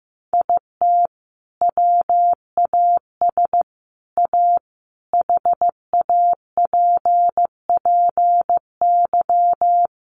Morse code, heard as a keyed tone, about 14 wpm, 700 Hz. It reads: IT WAS A HAPPY
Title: Sentences from Top 1100 Words 15wpm